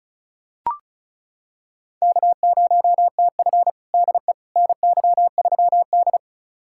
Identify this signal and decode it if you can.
Morse code, heard as a keyed tone, about 36 wpm, 700 Hz. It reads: K0TF DE NY3D
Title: USA Callsign DE Callsign - 4 Characters - 35wpm